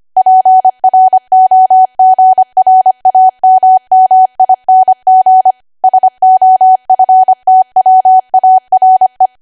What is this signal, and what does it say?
Morse code, heard as a keyed tone, about 25 wpm, 750 Hz. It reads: PROGRAMMING SOFTWARE
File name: Archive / 25 WPM / 25WPM